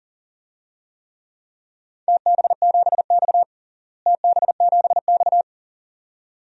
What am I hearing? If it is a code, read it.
Morse code, heard as a keyed tone, about 37 wpm, 700 Hz. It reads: TB7X TB7X
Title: DXCC-100 Call Signs - 4 Characters - Repeated 2x - 40wpm